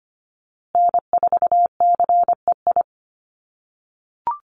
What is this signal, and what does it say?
Morse code, heard as a keyed tone, about 25 wpm, 700 Hz. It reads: N4CEI
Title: USA Call Signs Repeated 2x 25wpm